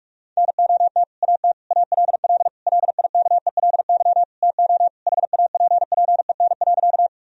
Morse code, heard as a keyed tone, about 41 wpm, 700 Hz. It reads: NOT AT ALL LIKELY TO HAPPEN.
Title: Sentences from Top 1000 Words - Review Part 4 - 45wpm